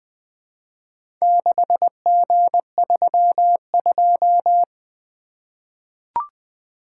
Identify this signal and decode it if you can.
Morse code, heard as a keyed tone, about 20 wpm, 700 Hz. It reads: 6G32
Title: Four Letter-Number Combinations - 20wpm